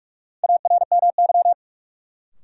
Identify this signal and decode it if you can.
Morse code, heard as a keyed tone, about 46 wpm, 700 Hz. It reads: ARMY
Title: Sentences from Top 800 Words - New Word Drill - 45wpm